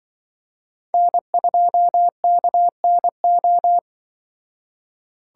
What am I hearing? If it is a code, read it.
Morse code, heard as a keyed tone, about 24 wpm, 700 Hz. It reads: N2KNO
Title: Speed Racing - USA 1x3 Call Signs 20wpm